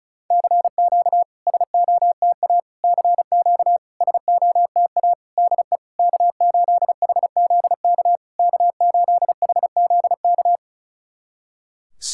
Morse code, heard as a keyed tone, about 36 wpm, 700 Hz. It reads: CQ SOTA CQ SOTA DE K8HZK K8HZK
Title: SOTA QSO Practice 35wpm